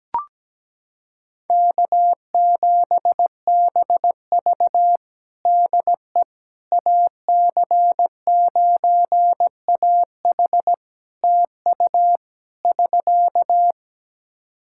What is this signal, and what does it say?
Morse code, heard as a keyed tone, about 17 wpm, 700 Hz. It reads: K7BV DE AC9AH TU <SK>
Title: Practice QSO - Exchange 4 - Ending - 17wpm